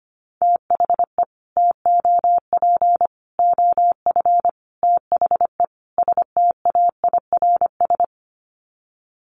Morse code, heard as a keyed tone, about 25 wpm, 700 Hz. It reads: THE TOP OF THE STAIRS
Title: Sets of 5 Words Encore 25wpm